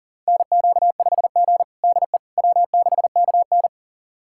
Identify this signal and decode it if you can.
Morse code, heard as a keyed tone, about 37 wpm, 700 Hz. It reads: NQ5G DE W6KN
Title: USA Callsign DE Callsign - 4 Characters - 40wpm